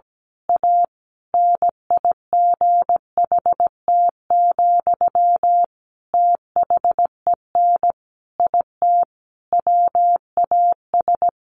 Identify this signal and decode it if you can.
Morse code, heard as a keyed tone, about 17 wpm, 700 Hz. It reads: A NIGHT, THEN IT WAS
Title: A Sound Of Thunder 17wpm